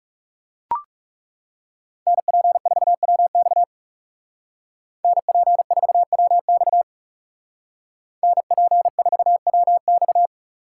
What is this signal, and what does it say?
Morse code, heard as a keyed tone, about 40 wpm, 700 Hz. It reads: NP4WX NP4WX NP4WX